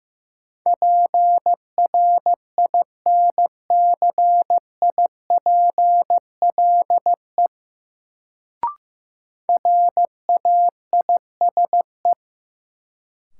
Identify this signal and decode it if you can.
Morse code, heard as a keyed tone, about 15 wpm, 700 Hz. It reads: PRINCIPLE RAISE
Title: Sentences from Top 1400 Words - New Word Drill Alt - 15wpm